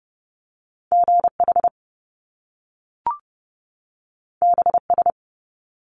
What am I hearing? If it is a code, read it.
Morse code, heard as a keyed tone, about 30 wpm, 700 Hz. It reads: GH BS
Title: Top Two Letter Combinations in Words 30wpm